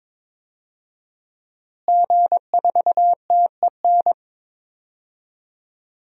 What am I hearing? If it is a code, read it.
Morse code, heard as a keyed tone, about 22 wpm, 700 Hz. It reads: G4TEN
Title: DXCC-100 Call Signs - 5 Characters - Repeated 2x - 22wpm